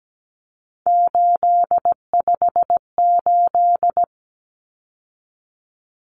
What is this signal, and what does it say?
Morse code, heard as a keyed tone, about 17 wpm, 700 Hz. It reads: 858
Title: Numbers - 3-Digit 17wpm